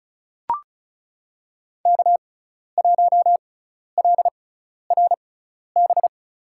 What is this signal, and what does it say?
Morse code, heard as a keyed tone, about 34 wpm, spaced as for 12 wpm, 700 Hz. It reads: K1LRB
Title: USA Call Signs - 5 Characters - 6x Character Spacing - 35wpm